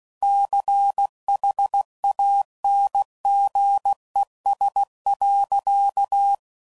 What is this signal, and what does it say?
Morse code, heard as a keyed tone, about 16 wpm, 800 Hz. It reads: CHANGES.
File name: CW Code Groups @15WPM / Technical Subjects / Reactance